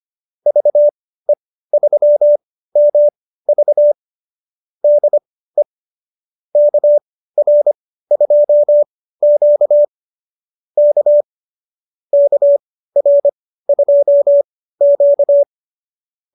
Morse code, heard as a keyed tone, about 25 wpm, spaced as for 15 wpm, 590 Hz. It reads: VE3MV DE KR2Q K KR2Q